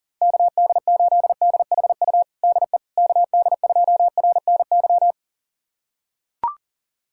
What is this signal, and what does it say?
Morse code, heard as a keyed tone, about 40 wpm, 700 Hz. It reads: KD8DHU DE KD2RNY
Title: USA Callsign DE Callsign 40wpm